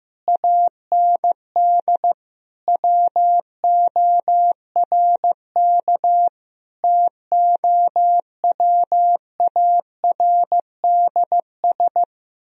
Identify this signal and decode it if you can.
Morse code, heard as a keyed tone, about 15 wpm, 700 Hz. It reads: AND WORK TOWARDS
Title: Sentences from Top 1000 Words - Review Part 3 - 15wpm